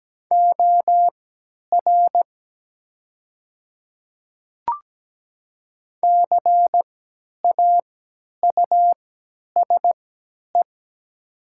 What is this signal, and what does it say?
Morse code, heard as a keyed tone, about 17 wpm, spaced as for 10 wpm, 700 Hz. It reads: OR CAUSE
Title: 3x Character Spacing - Top 300 Words - 17wpm